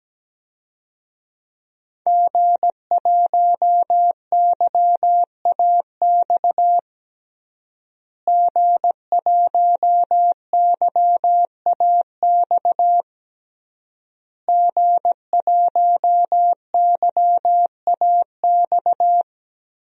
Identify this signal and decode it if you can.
Morse code, heard as a keyed tone, about 17 wpm, 700 Hz. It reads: G1YAX G1YAX G1YAX
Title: DXCC-100 Call Signs - 5 Characters - Repeated 3x - 17wpm